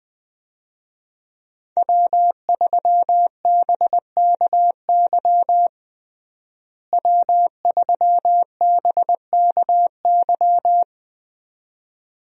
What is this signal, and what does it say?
Morse code, heard as a keyed tone, about 21 wpm, 700 Hz. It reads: W3BKY W3BKY
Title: USA Call Signs - 5 Characters - Repeated 2x - 20wpm